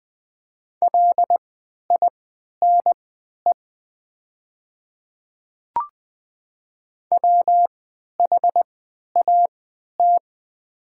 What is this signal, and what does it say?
Morse code, heard as a keyed tone, about 20 wpm, spaced as for 11 wpm, 700 Hz. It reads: LINE WHAT
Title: 3x Character Spacing - Top 200 Words - 20wpm